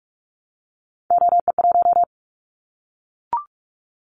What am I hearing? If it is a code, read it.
Morse code, heard as a keyed tone, about 44 wpm, 700 Hz. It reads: OE1
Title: Callsign Prefixes for Top 25 DXCC - 45wpm